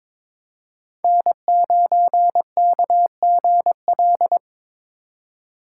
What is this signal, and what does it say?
Morse code, heard as a keyed tone, about 21 wpm, 700 Hz. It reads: N9KGL
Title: USA Call Signs - 5 Characters - 3x Character Spacing - 22wpm